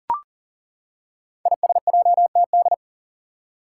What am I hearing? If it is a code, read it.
Morse code, heard as a keyed tone, about 39 wpm, 700 Hz. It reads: ISJTD
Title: Five Challenging Letter-Number Combinations 40wpm